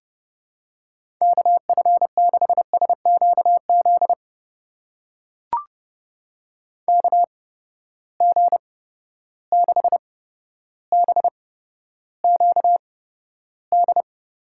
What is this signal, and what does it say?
Morse code, heard as a keyed tone, about 30 wpm, spaced as for 13 wpm, 700 Hz. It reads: KF6SQZ KG6BQD
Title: USA Call Signs - 6 Characters - 8x Character Spacing - 30wpm